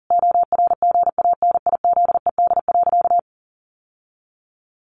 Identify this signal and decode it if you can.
Morse code, heard as a keyed tone, about 40 wpm, 700 Hz. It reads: ORGANIZED.
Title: Sentences from Top 1300 Words 40wpm